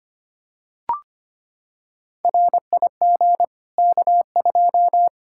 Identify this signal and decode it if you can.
Morse code, heard as a keyed tone, about 25 wpm, 700 Hz. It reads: RIG K2
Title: Practice QSO - Exchange 2 - Info - 25wpm